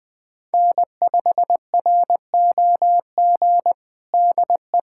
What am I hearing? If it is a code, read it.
Morse code, heard as a keyed tone, about 20 wpm, 700 Hz. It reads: N5ROG DE
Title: USA Callsign DE Callsign - 5 Characters - 20wpm